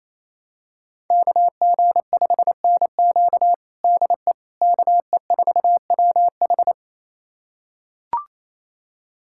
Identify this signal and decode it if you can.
Morse code, heard as a keyed tone, about 27 wpm, 700 Hz. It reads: KG5NQ DE KE4WH
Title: USA Callsign DE Callsign 28wpm